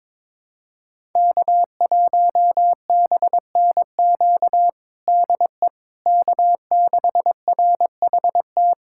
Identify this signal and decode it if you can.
Morse code, heard as a keyed tone, about 22 wpm, 700 Hz. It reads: K1BNQ DE K6RHT
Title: K1USN SST QSO Practice 22wpm